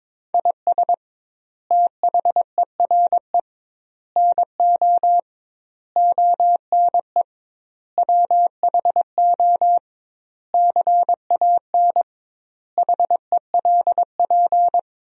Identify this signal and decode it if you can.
Morse code, heard as a keyed tone, about 22 wpm, 700 Hz. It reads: IS THERE NO ONE WHO CAN HELP
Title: 2x Word-Spacing - Sentences from Top 200 Words - 22wpm